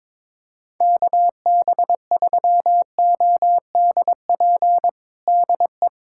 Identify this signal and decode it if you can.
Morse code, heard as a keyed tone, about 21 wpm, 700 Hz. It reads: KB3ODP DE
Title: USA Callsign DE Callsign 22wpm